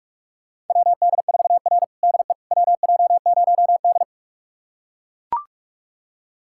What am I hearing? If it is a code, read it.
Morse code, heard as a keyed tone, about 41 wpm, 700 Hz. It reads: WD4R DE WJ0D
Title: USA Callsign DE Callsign - 4 Characters - 45wpm